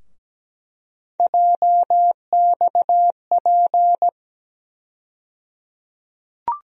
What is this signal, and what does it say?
Morse code, heard as a keyed tone, about 17 wpm, 700 Hz. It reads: JXP
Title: USA Call Sign Postfixes - 3 Letters 17wpm